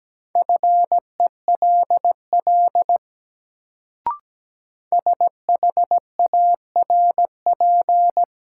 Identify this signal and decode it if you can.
Morse code, heard as a keyed tone, about 17 wpm, 700 Hz. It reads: FELL SHARP
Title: Sentences from Top 1600 Words - New Word Drill - 17wpm